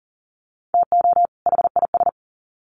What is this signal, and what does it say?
Morse code, heard as a keyed tone, about 40 wpm, 700 Hz. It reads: TO HIS
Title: Sets of 2 Words 40wpm